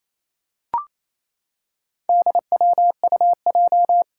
Morse code, heard as a keyed tone, about 27 wpm, 700 Hz. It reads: DWUJ
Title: Four Challenging Letter-Number Combinations 28wpm